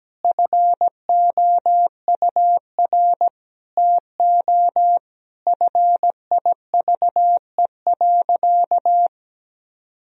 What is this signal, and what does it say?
Morse code, heard as a keyed tone, about 17 wpm, 700 Hz. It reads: FOUR TO FIVE.
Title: Sentences from Top 1000 Words - Review Part 2 - 17wpm